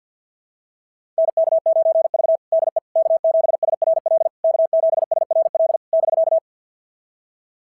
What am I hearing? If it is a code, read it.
Morse code, heard as a keyed tone, about 53 wpm, 690 Hz. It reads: NK9V DE K7SRL K7SRL <BK>
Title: SOTA QSO Practice 50wpm